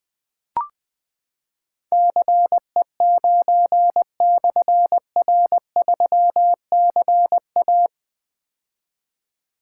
Morse code, heard as a keyed tone, about 20 wpm, 700 Hz. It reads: CE9/R3CA
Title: Difficult Call Signs 20wpm